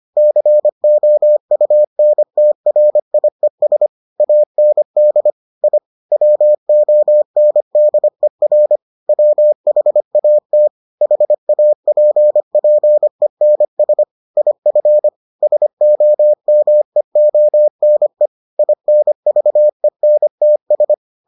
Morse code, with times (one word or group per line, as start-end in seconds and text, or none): 0.2-3.9 COUNTRIES
4.2-5.3 AND
5.6-5.8 I
6.1-8.8 WONDER
9.1-10.7 WHAT
11.0-14.0 HAPPENS
14.4-15.1 IF
15.4-18.3 SOMEONE
18.6-20.9 INVENTS